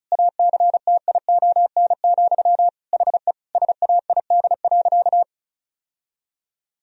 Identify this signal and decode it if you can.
Morse code, heard as a keyed tone, about 34 wpm, 700 Hz. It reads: ACTION, HE SAID.
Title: Sentences from Top 1000 Words - Review Part 4 - 35wpm